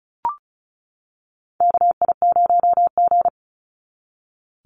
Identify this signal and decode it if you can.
Morse code, heard as a keyed tone, about 34 wpm, 700 Hz. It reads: KI0G
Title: NA SOTA Chaser Callsigns 35wpm